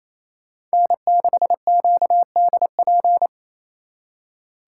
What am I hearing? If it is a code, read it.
Morse code, heard as a keyed tone, about 35 wpm, 700 Hz. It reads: N6QDP
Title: Speed Racing - USA 1x3 Call Signs 28wpm